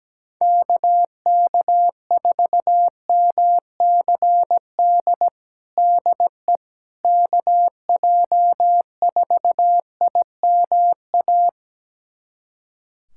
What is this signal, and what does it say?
Morse code, heard as a keyed tone, about 17 wpm, 700 Hz. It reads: KK4MCD DE KJ4IMA
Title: USA Callsign DE Callsign - 6 Characters - 17wpm